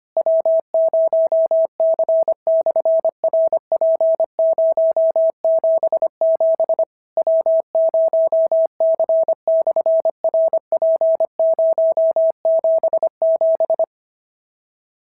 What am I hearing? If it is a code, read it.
Morse code, heard as a keyed tone, about 24 wpm, 650 Hz. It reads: W0C/RP077 W0C/RP077
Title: SOTA QSO Practice 25wpm